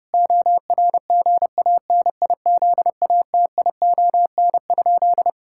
Morse code, heard as a keyed tone, about 30 wpm, 700 Hz. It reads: ORGANIZATION?
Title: Sentences from Top 1000 Words - Review Part 5 - 30wpm